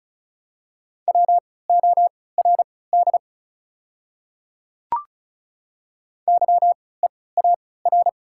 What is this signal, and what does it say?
Morse code, heard as a keyed tone, about 36 wpm, spaced as for 20 wpm, 700 Hz. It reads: WORD YEAR
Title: 3x Character Spacing - Top 200 Words - 35wpm